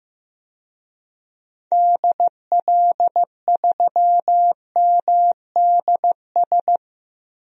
Morse code, heard as a keyed tone, about 15 wpm, 690 Hz. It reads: DL3MDS
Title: DXCC-100 Call Signs - 6 Characters - 15wpm